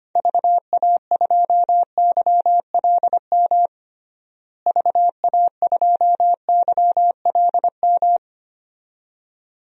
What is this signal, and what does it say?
Morse code, heard as a keyed tone, about 25 wpm, 700 Hz. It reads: VA2YLM VA2YLM
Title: DXCC-100 Call Signs - 6 Characters - Repeated 2x - 25wpm